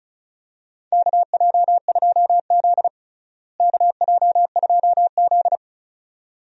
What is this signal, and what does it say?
Morse code, heard as a keyed tone, about 34 wpm, 700 Hz. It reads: KJ2Z KJ2Z
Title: USA Call Signs - 4 Characters - Repeated 2x - 35wpm